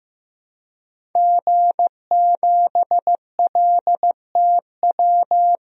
Morse code, heard as a keyed tone, about 15 wpm, 700 Hz. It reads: G7LTW
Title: DXCC-100 Call Signs - 5 Characters - 15wpm